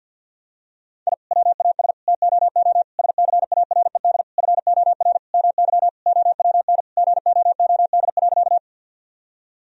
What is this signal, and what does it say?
Morse code, heard as a keyed tone, about 50 wpm, 700 Hz. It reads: I WAS TOO SCARED FOR MY OWN GOOD.
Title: Sentences from Top 1000 Words - Review Part 4 - 50wpm